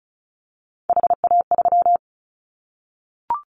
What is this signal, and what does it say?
Morse code, heard as a keyed tone, about 36 wpm, 700 Hz. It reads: HA3
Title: Callsign Prefixes for Top 25 DXCC - 35wpm